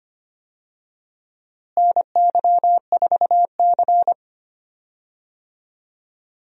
Morse code, heard as a keyed tone, about 25 wpm, 700 Hz. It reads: NY4C
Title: USA Call Signs - 4 Characters - Repeated 3x - 25wpm